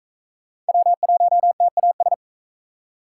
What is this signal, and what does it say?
Morse code, heard as a keyed tone, about 37 wpm, 700 Hz. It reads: W1TAS